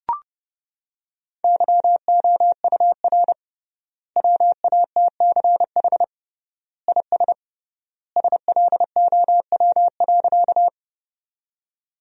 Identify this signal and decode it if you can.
Morse code, heard as a keyed tone, about 32 wpm, 700 Hz. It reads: YOUR WATCH IS SLOW.
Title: 3x Word-Spacing - Sentences from Top 400 Words - 30wpm